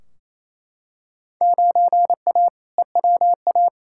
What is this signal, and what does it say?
Morse code, heard as a keyed tone, about 28 wpm, 700 Hz. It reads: 9A EWA
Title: Field Day Exchanges Encore - 28wpm